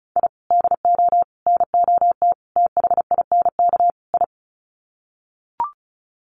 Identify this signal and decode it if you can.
Morse code, heard as a keyed tone, about 34 wpm, 700 Hz. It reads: I DO NOT THINK I
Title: Sets of 5 Words 35wpm